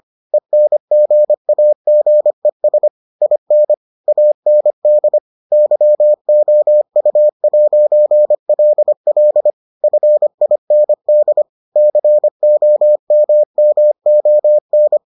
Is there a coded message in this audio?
Yes: Morse code from 0.3 to 15.0 s, heard as a keyed tone, about 25 wpm, 600 Hz. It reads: ENGAGES IN AND YOU'LL FIND COMMON